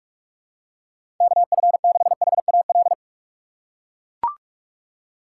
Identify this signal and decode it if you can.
Morse code, heard as a keyed tone, about 43 wpm, 700 Hz. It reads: KF6HAL
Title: USA Call Signs - 6 Characters - 4x Character Spacing - 45wpm